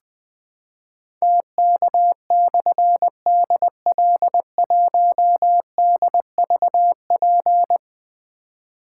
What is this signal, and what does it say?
Morse code, heard as a keyed tone, about 20 wpm, 700 Hz. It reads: TK/DL1DVP
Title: Difficult Call Signs 20wpm